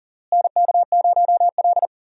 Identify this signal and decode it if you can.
Morse code, heard as a keyed tone, about 40 wpm, 700 Hz. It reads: NK0L
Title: USA Call Signs - 4 Characters - Repeated 3x - 40wpm